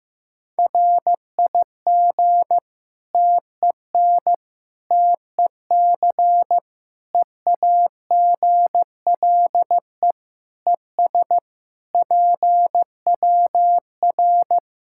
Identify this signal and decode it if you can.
Morse code, heard as a keyed tone, about 15 wpm, 700 Hz. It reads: RIG TEN TEC EAGLE ES PWR
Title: K1USN SST QSO Practice 15wpm